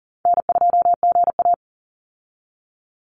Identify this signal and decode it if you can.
Morse code, heard as a keyed tone, about 40 wpm, 700 Hz. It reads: N2GA
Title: Easy Call Signs Repeated 3x - 40wpm